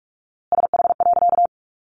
Morse code, heard as a keyed tone, about 44 wpm, 700 Hz. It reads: SH.
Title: A Sound Of Thunder 45wpm